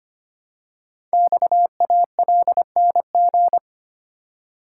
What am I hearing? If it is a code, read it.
Morse code, heard as a keyed tone, about 25 wpm, 700 Hz. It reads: XALNG
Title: Code Groups - 5 Letters - 25wpm